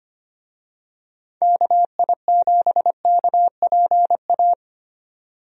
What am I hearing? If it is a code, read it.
Morse code, heard as a keyed tone, about 25 wpm, 700 Hz. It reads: KI7KPA
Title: USA Call Signs - 6 Characters - 25wpm